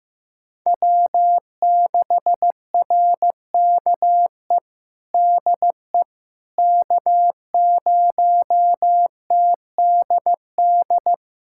Morse code, heard as a keyed tone, about 16 wpm, 700 Hz. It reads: W6RKE DE K0TDD
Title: USA Callsign DE Callsign - 5 Characters - 15wpm